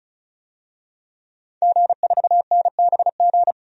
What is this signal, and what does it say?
Morse code, heard as a keyed tone, about 36 wpm, 700 Hz. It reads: G4NBG
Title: DXCC-100 Call Signs - 5 Characters - 35wpm